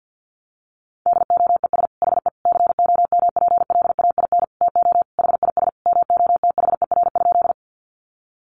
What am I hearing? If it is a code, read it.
Morse code, heard as a keyed tone, about 50 wpm, 700 Hz. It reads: DOES HE COMPLAIN TO HIS MOTHER?
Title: Sentences from Top 1300 Words 50wpm